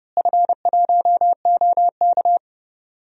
Morse code, heard as a keyed tone, about 32 wpm, 700 Hz. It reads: F1OK
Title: DXCC-100 Call Signs - 4 Characters - Repeated 3x - 30wpm